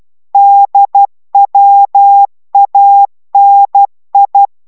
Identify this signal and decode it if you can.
Morse code, heard as a keyed tone, about 12 wpm, 800 Hz. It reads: DWANI